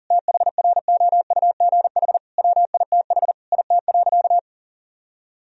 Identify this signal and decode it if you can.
Morse code, heard as a keyed tone, about 40 wpm, 700 Hz. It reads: THROUGH WITH IT.
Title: Sentences from Top 1000 Words - Review Part 3 - 40wpm